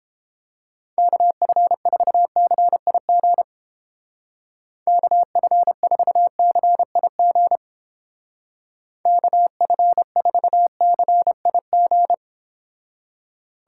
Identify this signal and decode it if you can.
Morse code, heard as a keyed tone, about 29 wpm, 700 Hz. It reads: KF4CIG KF4CIG KF4CIG